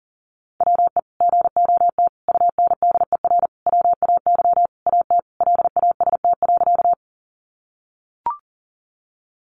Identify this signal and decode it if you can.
Morse code, heard as a keyed tone, about 40 wpm, 700 Hz. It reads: WE GOT UNDER WAY AT LAST.
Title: Sentences from Top 300 Words 40wpm